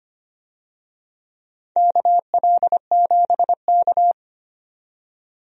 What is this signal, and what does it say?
Morse code, heard as a keyed tone, about 25 wpm, 700 Hz. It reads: KL7K
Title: DXCC-100 Call Signs - 4 Characters - 25wpm